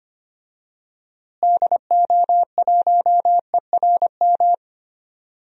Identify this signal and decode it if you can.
Morse code, heard as a keyed tone, about 25 wpm, 700 Hz. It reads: DO1ERM
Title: DXCC-100 Call Signs - 6 Characters - Repeated 3x - 25wpm